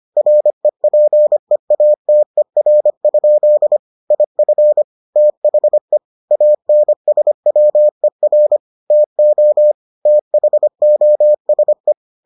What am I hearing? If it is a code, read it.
Morse code, heard as a keyed tone, about 25 wpm, 600 Hz. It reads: REPEATER? IF THE ANSWER TO THOSE